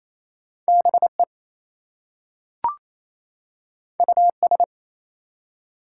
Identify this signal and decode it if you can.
Morse code, heard as a keyed tone, about 27 wpm, 700 Hz. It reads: BE US
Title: Two Letter Words - 28wpm